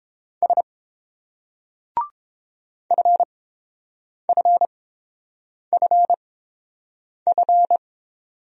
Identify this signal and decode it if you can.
Morse code, heard as a keyed tone, about 29 wpm, spaced as for 7 wpm, 700 Hz. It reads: S FFFF